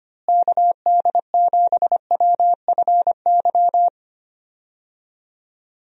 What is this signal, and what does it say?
Morse code, heard as a keyed tone, about 25 wpm, 700 Hz. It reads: KD7WFY
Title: USA Call Signs - 6 Characters - Repeated 2x - 25wpm